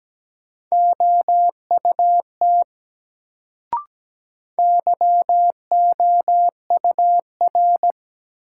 Morse code, heard as a keyed tone, about 17 wpm, 700 Hz. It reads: OUT YOUR
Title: Sets of 2 Words - New Word Drill - 17wpm